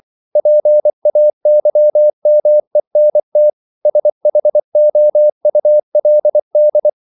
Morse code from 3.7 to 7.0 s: SHOULD

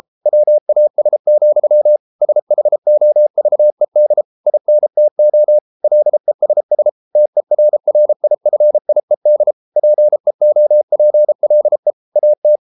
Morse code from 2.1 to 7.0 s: SHOVED INTO LESS

